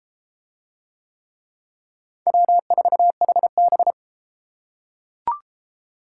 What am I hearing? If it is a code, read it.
Morse code, heard as a keyed tone, about 32 wpm, 700 Hz. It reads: W4HB